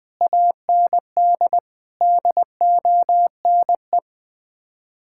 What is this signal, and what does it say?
Morse code, heard as a keyed tone, about 20 wpm, 700 Hz. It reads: AND DONE
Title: Sets of 6 Words - 20wpm